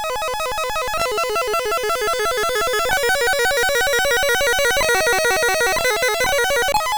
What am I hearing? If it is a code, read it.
There is no code.